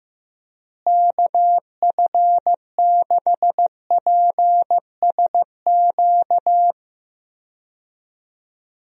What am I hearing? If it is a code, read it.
Morse code, heard as a keyed tone, about 15 wpm, 700 Hz. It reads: KF6PSQ
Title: USA Call Signs - 6 Characters - Repeated 2x - 15wpm